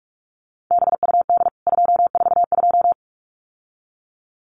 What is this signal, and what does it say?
Morse code, heard as a keyed tone, about 44 wpm, 700 Hz. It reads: BUD 242
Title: CWT Exchange 45wpm